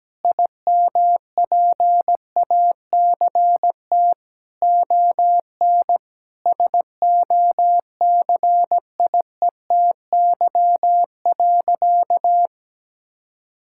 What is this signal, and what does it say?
Morse code, heard as a keyed tone, about 17 wpm, 700 Hz. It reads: IMPACT ON SOCIETY.
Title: Sentences from Top 1000 Words - Review Part 6 - 17wpm